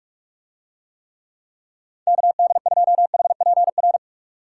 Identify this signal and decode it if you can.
Morse code, heard as a keyed tone, about 49 wpm, 700 Hz. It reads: KD2HPR